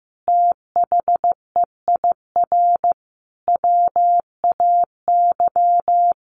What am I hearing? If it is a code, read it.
Morse code, heard as a keyed tone, about 15 wpm, 700 Hz. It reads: THEIR WAY
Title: Sentences from Top 1500 Words 15wpm